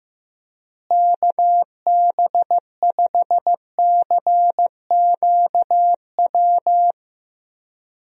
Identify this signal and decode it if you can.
Morse code, heard as a keyed tone, about 15 wpm, 700 Hz. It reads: KB5CQW
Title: USA Call Signs - 6 Characters - Repeated 2x - 15wpm